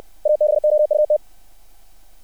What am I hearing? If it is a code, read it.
Morse code, heard as a keyed tone, about 60 wpm, 600 Hz. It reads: ABOUT
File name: archived / CRSnet